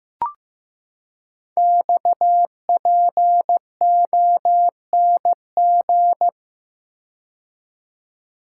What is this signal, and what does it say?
Morse code, heard as a keyed tone, about 15 wpm, 700 Hz. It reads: XPONG
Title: Code Groups - 5 Letters - 15wpm